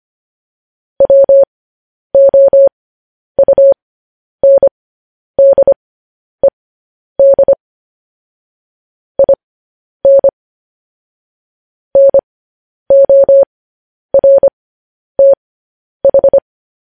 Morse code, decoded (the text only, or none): WOUNDED IN NORTH